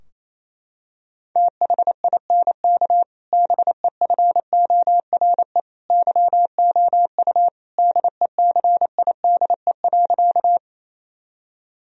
Morse code, heard as a keyed tone, about 28 wpm, 700 Hz. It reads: THINK BEFORE YOU DECIDE.
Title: Sentences from Top 1000 Words - Review Part 3 - 28wpm